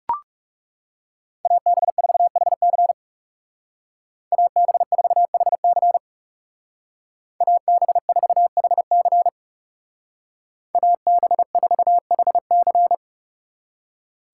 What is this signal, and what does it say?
Morse code, heard as a keyed tone, about 37 wpm, 700 Hz. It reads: AB4HC AB4HC AB4HC AB4HC
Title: Speed Racing - USA 2x2 Call Signs 30wpm